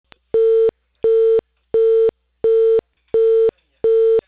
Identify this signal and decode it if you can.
Busy tone (China, 450 Hz).